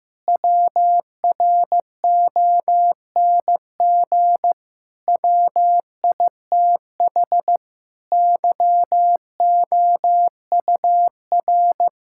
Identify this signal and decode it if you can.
Morse code, heard as a keyed tone, about 15 wpm, 700 Hz. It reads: WRONG WITH YOUR
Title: Sentences from Top 1000 Words - Review Part 5 - 15wpm